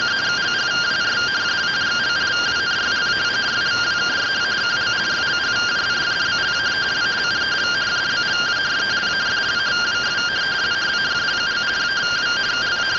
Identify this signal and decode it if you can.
Two-tone teleprinter signal, about 170 Hz shift.